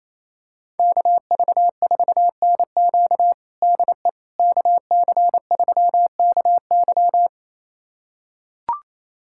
Morse code, heard as a keyed tone, about 27 wpm, 700 Hz. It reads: KV4NQ DE KC3KY
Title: USA Callsign DE Callsign 28wpm